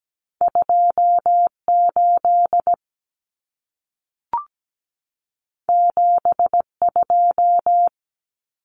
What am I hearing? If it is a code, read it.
Morse code, heard as a keyed tone, about 17 wpm, 700 Hz. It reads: 28 72
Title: Numbers - 2-Digit 17wpm